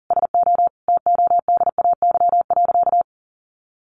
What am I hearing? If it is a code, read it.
Morse code, heard as a keyed tone, about 37 wpm, 700 Hz. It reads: SO TODAY.
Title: Sentences from Top 1500 Words 40wpm